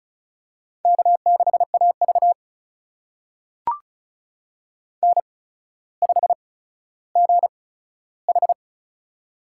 Morse code, heard as a keyed tone, about 36 wpm, spaced as for 15 wpm, 700 Hz. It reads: K6AV N5GH
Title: USA Call Signs - 4 Characters - 8x Character Spacing - 35wpm